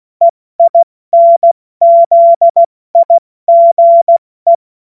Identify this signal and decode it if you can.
Morse code, heard as a keyed tone, about 16 wpm, 680 Hz. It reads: EINZIGE